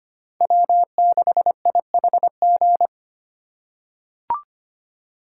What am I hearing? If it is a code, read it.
Morse code, heard as a keyed tone, about 25 wpm, 700 Hz. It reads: W6IHG
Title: K1USN SST QSO Practice - 25wpm